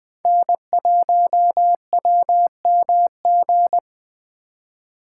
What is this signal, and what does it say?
Morse code, heard as a keyed tone, about 20 wpm, 700 Hz. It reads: N1WMG
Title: USA Call Signs - 5 Characters - 3x Character Spacing - 20wpm